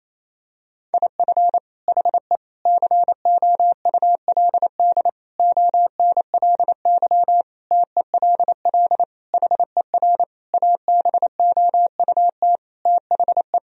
Morse code, heard as a keyed tone, about 28 wpm, 700 Hz. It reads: IF HE COULD ONLY TELL HER ABOUT THE
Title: Sentences from Top 1000 Words - Review Part 1 - 28wpm